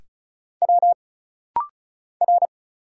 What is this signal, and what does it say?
Morse code, heard as a keyed tone, about 34 wpm, 700 Hz. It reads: W R
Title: Single Letters - Rapid-Fire 35wpm